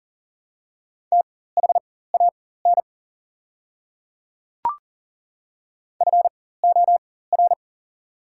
Morse code, heard as a keyed tone, about 40 wpm, spaced as for 19 wpm, 700 Hz. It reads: THAN FOR
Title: 4x Character Spacing - Top 100 Words - 40wpm